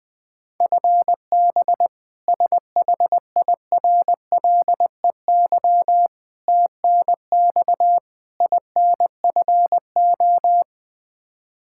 Morse code, heard as a keyed tone, about 20 wpm, 700 Hz. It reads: FB SHIRLEY TNX INFO
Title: K1USN SST QSO Practice 20wpm